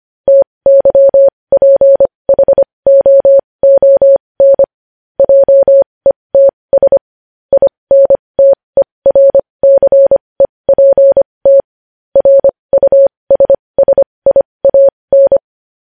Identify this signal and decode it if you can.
Morse code, heard as a keyed tone, about 25 wpm, spaced as for 20 wpm, 550 Hz. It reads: TYPHOON JETS INTERCEPT RUSSIAN
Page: news-headlines-in-morse-code-at-20-wpm